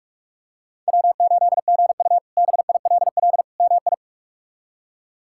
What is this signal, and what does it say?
Morse code, heard as a keyed tone, about 46 wpm, 700 Hz. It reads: W8GU BILL MI
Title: K1USN SST QSO Practice - 45wpm